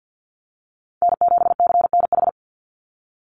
Morse code, heard as a keyed tone, about 49 wpm, 700 Hz. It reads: N7CNH
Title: NA SOTA Chaser Callsigns 50wpm